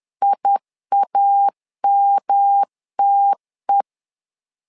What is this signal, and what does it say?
Morse code, heard as a keyed tone, about 10 wpm, 800 Hz. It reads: IAMTE
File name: LEZIONI / 3 MWRO / 3 MW 10wpm 1959